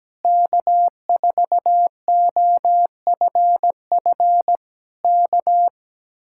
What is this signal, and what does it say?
Morse code, heard as a keyed tone, about 17 wpm, 700 Hz. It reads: K4OFF K
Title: Speed-Racing Common QSO Phrases - 17wpm